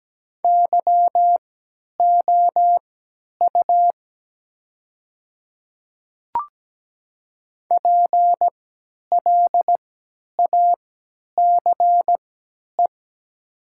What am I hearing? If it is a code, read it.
Morse code, heard as a keyed tone, about 17 wpm, spaced as for 10 wpm, 700 Hz. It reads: YOU PLACE
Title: 3x Character Spacing - Top 500 Words - 17wpm